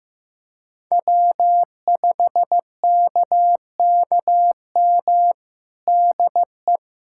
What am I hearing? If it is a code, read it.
Morse code, heard as a keyed tone, about 15 wpm, 700 Hz. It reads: W5KKM DE
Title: USA Callsign DE Callsign - 5 Characters - 15wpm